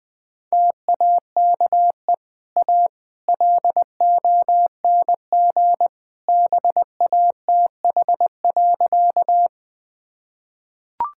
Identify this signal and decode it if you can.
Morse code, heard as a keyed tone, about 21 wpm, 700 Hz. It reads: TAKE A LONG BATH.
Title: Easy Sentences with Hints - 20wpm